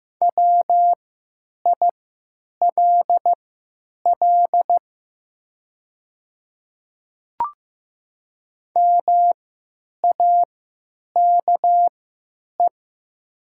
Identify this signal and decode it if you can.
Morse code, heard as a keyed tone, about 15 wpm, spaced as for 9 wpm, 700 Hz. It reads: WILL MAKE
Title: 3x Character Spacing - Top 100 Words - 15wpm